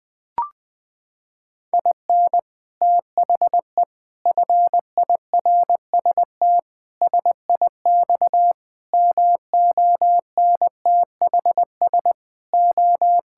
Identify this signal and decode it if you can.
Morse code, heard as a keyed tone, about 21 wpm, 700 Hz. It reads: IN THE FIRST SIX MONTHS O
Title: Sentences with 6-Word Phrases - 20wpm